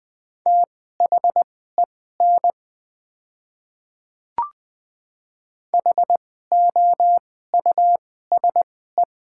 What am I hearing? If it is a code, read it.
Morse code, heard as a keyed tone, about 20 wpm, spaced as for 14 wpm, 700 Hz. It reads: THEN HOUSE
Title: 2x Character Spacing - Top 200 Words - 20wpm